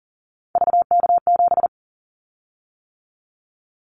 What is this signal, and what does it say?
Morse code, heard as a keyed tone, about 40 wpm, 700 Hz. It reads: VK7
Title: Callsign Prefixes for Top 100 DXCC - 40wpm